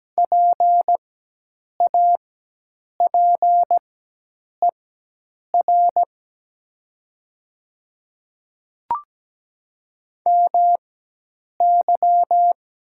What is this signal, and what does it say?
Morse code, heard as a keyed tone, about 17 wpm, spaced as for 8 wpm, 690 Hz. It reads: PAPER MY
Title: 4x Character Spacing - Top 300 Words - 17wpm